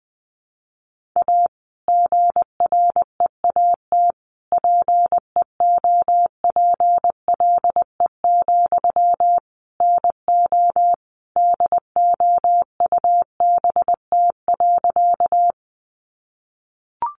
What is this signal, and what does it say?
Morse code, heard as a keyed tone, about 20 wpm, 700 Hz. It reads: A GREAT PEOPLE, NO DOUBT.
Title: Sentences from Top 1500 Words 20wpm